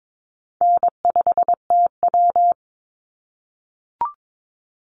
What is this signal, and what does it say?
Morse code, heard as a keyed tone, about 22 wpm, 700 Hz. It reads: N5TW
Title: Easy Call Signs - 22wpm